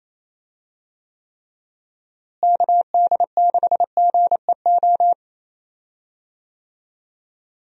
Morse code, heard as a keyed tone, about 27 wpm, 700 Hz. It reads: KD6GEO
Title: USA Call Signs - 6 Characters - Repeated 2x - 28wpm